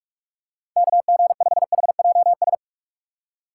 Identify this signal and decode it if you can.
Morse code, heard as a keyed tone, about 46 wpm, 700 Hz. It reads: KG5HJS